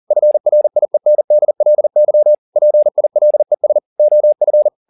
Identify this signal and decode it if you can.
Morse code, heard as a keyed tone, about 40 wpm, 600 Hz. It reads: FRIENDLY PILES OF